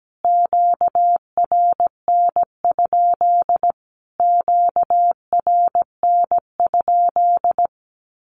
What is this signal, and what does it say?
Morse code, heard as a keyed tone, about 17 wpm, 700 Hz. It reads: QRN? QRN?
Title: Q-Codes Repeated 3x 17wpm